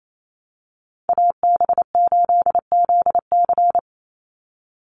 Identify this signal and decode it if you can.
Morse code, heard as a keyed tone, about 28 wpm, 700 Hz. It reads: AB8ZC
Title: USA Call Signs 28wpm